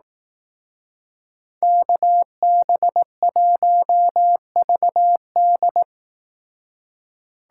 Morse code, heard as a keyed tone, about 18 wpm, 700 Hz. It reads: KB1VD
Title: Speed Racing - USA 2x2 Call Signs 15wpm